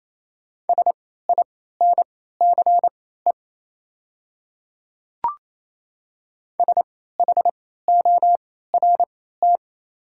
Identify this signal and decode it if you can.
Morse code, heard as a keyed tone, about 28 wpm, spaced as for 16 wpm, 700 Hz. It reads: SINCE SHORT
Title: 3x Character Spacing - Top 500 Words - 28wpm